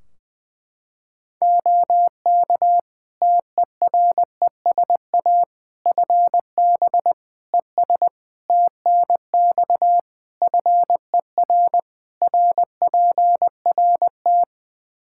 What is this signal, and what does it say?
Morse code, heard as a keyed tone, about 21 wpm, 700 Hz. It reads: OK TERESA FB ES TNX FER RPRT
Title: Practice QSO - Exchange 2 - Info - 20wpm